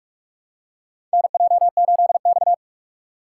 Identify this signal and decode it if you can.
Morse code, heard as a keyed tone, about 46 wpm, 700 Hz. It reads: NJ8X
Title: USA Call Signs - 4 Characters - 4x Character Spacing - 45wpm